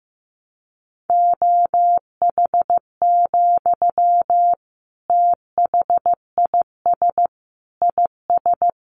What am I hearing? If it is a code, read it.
Morse code, heard as a keyed tone, about 14 wpm, 700 Hz. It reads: OH, THIS IS
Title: Sentences from Top 1100 Words 15wpm